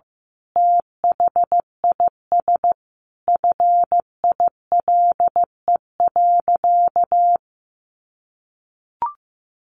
Morse code, heard as a keyed tone, about 15 wpm, 700 Hz. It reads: THIS FILE.
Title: Sentences from Top 1200 Words 15wpm